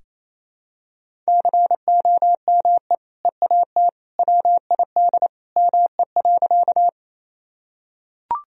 Morse code, heard as a keyed tone, about 28 wpm, 700 Hz. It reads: COME EAT WID ME.
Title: Sentences from Top 300 Words and Abbreviations 28wpm